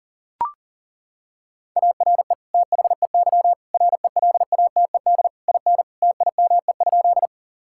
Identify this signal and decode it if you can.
Morse code, heard as a keyed tone, about 40 wpm, 700 Hz. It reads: ARE THEY RELATED IN TIME?
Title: Sentences from Top 1000 Words - Review Part 5 - 40wpm